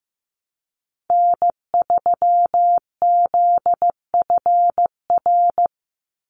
Morse code, heard as a keyed tone, about 15 wpm, 700 Hz. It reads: N3ZFR
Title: USA Call Signs 15wpm